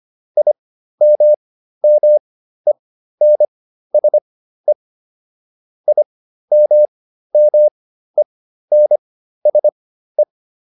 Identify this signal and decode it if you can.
Morse code, heard as a keyed tone, about 25 wpm, spaced as for 13 wpm, 600 Hz. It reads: IMMENSE IMMENSE
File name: IM7 13